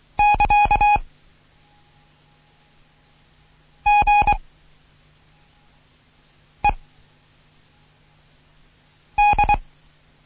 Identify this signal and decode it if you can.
Morse code, heard as a keyed tone, about 23 wpm, 830 Hz. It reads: <KA> G E D
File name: Morsen / Kurs A 16